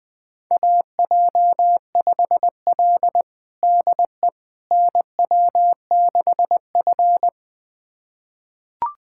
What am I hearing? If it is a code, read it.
Morse code, heard as a keyed tone, about 20 wpm, 700 Hz. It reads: AJ5L DE NW6F
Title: USA Callsign DE Callsign - 4 Characters - 20wpm